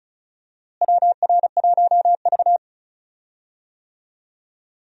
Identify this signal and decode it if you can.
Morse code, heard as a keyed tone, about 34 wpm, 700 Hz. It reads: WR1V
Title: USA Call Signs - 4 Characters - Repeated 2x - 35wpm